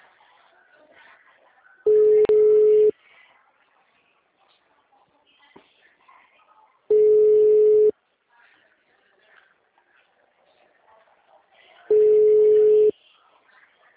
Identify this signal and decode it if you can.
Ringback tone (Europe, 425 Hz).